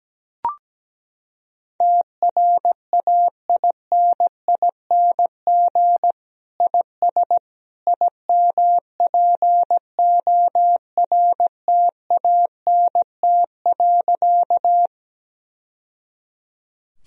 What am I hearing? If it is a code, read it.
Morse code, heard as a keyed tone, about 17 wpm, 700 Hz. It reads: TRAINING IS IMPORTANT.
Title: Sentences from Top 1000 Words - Review Part 4 - 17wpm